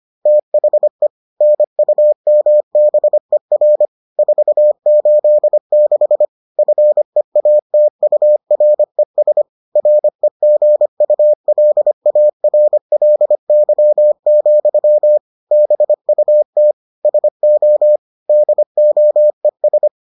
Morse code, heard as a keyed tone, about 25 wpm, 600 Hz. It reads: THE NUMBER 486 FEATURES REGULARLY, BUT SO DOES